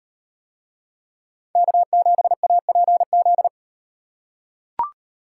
Morse code, heard as a keyed tone, about 38 wpm, 700 Hz. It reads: K7APZ